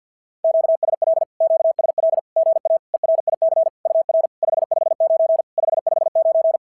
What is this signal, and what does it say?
Morse code, heard as a keyed tone, about 50 wpm, 650 Hz. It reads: QSL QSL GA ERIC UR 559 559
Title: SOTA QSO Practice 50wpm